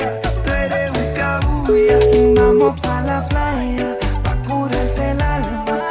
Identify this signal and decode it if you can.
Ringback tone (Europe, 425 Hz), with noise.